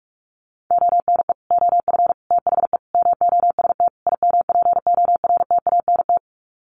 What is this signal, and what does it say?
Morse code, heard as a keyed tone, about 44 wpm, 700 Hz. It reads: ONE OF THE MOST IMPORTANT
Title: Sets of 5 Words 45wpm